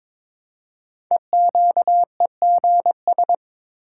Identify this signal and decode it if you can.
Morse code, heard as a keyed tone, about 22 wpm, 700 Hz. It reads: EQEGS
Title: Code Groups - 5 Letters - 22wpm